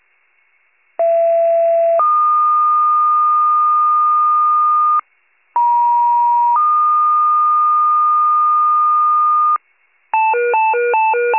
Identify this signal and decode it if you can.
Two-tone page, heard as a one-second tone, then a three-second tone, about 670 then 1220 Hz, about 950 then 1220 Hz. It is heard from 1.0 to 9.6 s.